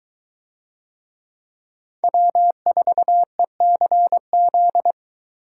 Morse code, heard as a keyed tone, about 22 wpm, 700 Hz. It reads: W4ECZ